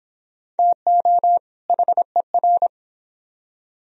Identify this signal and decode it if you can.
Morse code, heard as a keyed tone, about 25 wpm, 700 Hz. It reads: TO HER